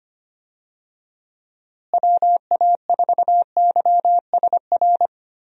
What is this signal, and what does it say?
Morse code, heard as a keyed tone, about 25 wpm, 700 Hz. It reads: WA4YSR
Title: USA Call Signs - 6 Characters - Repeated 3x - 25wpm